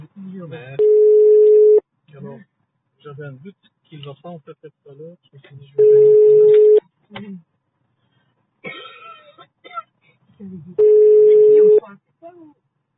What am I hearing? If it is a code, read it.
Ringback tone (Europe, 425 Hz), with speech.